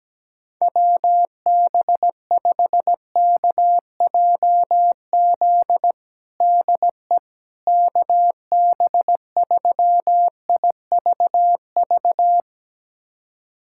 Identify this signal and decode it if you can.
Morse code, heard as a keyed tone, about 17 wpm, 700 Hz. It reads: WB5KJZ DE KB3IVV
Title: USA Callsign DE Callsign - 6 Characters - 17wpm